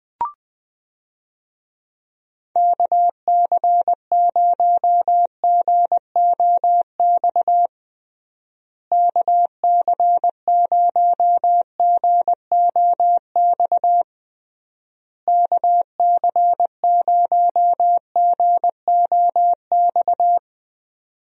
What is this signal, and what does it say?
Morse code, heard as a keyed tone, about 20 wpm, 700 Hz. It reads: KC0GOX KC0GOX KC0GOX
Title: USA Call Signs - 6 Characters - Repeated 3x - 20wpm